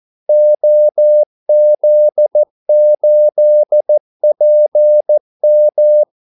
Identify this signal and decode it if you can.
Morse code, heard as a keyed tone, about 14 wpm, 600 Hz. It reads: OZ8PM
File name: Callsigns-14wpm